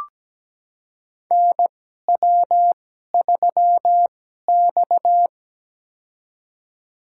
Morse code, heard as a keyed tone, about 17 wpm, spaced as for 12 wpm, 700 Hz. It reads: NW3X
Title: USA Call Signs - 2x Character Spacing - 17wpm